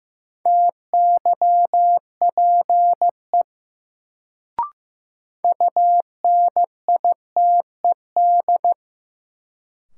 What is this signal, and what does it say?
Morse code, heard as a keyed tone, about 15 wpm, 700 Hz. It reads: TYPE UNITED
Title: Sentences from Top 600 Words - New Word Drill - 15wpm